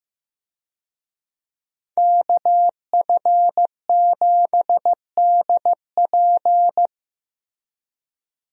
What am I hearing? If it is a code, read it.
Morse code, heard as a keyed tone, about 15 wpm, 700 Hz. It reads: KF7DP
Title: USA Call Signs - 5 Characters - Repeated 2x - 15wpm